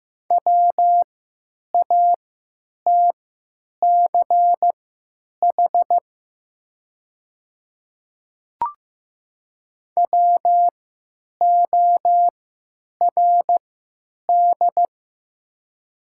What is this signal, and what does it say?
Morse code, heard as a keyed tone, about 15 wpm, spaced as for 9 wpm, 700 Hz. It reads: WATCH WORD
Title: 3x Character Spacing - Top 500 Words - 15wpm